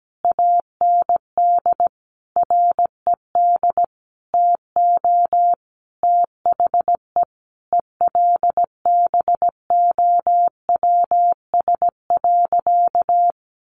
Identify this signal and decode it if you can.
Morse code, heard as a keyed tone, about 17 wpm, 700 Hz. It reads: AND RED TO THE ELBOWS.
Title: A Sound Of Thunder 17wpm